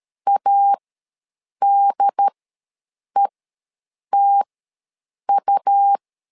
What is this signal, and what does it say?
Morse code, heard as a keyed tone, about 13 wpm, spaced as for 7 wpm, 800 Hz. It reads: ADETU
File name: LEZIONI / 2 UHDN / 2 UHD ws 12wpm 1722